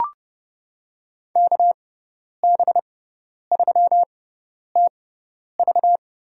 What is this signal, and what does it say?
Morse code, heard as a keyed tone, about 32 wpm, spaced as for 10 wpm, 700 Hz. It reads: KB3TV
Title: USA Call Signs - 6 Characters - 6x Character Spacing - 30wpm